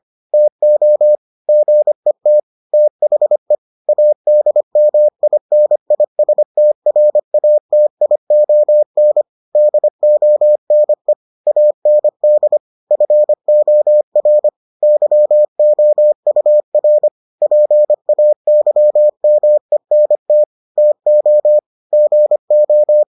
Morse code, heard as a keyed tone, about 25 wpm, 600 Hz. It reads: TO GET THE ADMINISTRATION DONE AND FOR YOUR PAYMENT TO GO